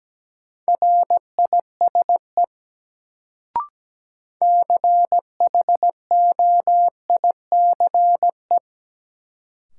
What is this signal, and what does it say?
Morse code, heard as a keyed tone, about 17 wpm, 700 Hz. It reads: RISE CHOICE
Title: Sentences from Top 900 Words - New Word Drill - 17wpm